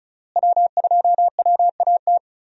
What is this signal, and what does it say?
Morse code, heard as a keyed tone, about 36 wpm, 700 Hz. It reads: W2WAT
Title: USA Call Signs - 5 Characters - Repeated 3x - 35wpm